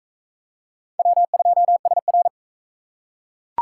Morse code, heard as a keyed tone, about 41 wpm, 700 Hz. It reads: W2SR